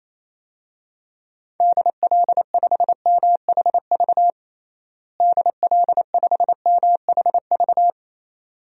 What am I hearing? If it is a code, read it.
Morse code, heard as a keyed tone, about 27 wpm, 700 Hz. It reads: DL5MHV DL5MHV
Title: DXCC-100 Call Signs - 6 Characters - Repeated 2x - 28wpm